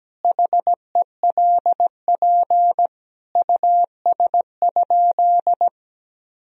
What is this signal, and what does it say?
Morse code, heard as a keyed tone, about 17 wpm, 700 Hz. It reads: HELP US?
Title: Sentences from Top 1000 Words - Review Part 3 - 17wpm